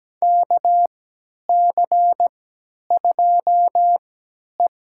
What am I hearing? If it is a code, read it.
Morse code, heard as a keyed tone, about 17 wpm, spaced as for 10 wpm, 700 Hz. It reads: KC2E
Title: USA Call Signs - 3x Character Spacing - 17wpm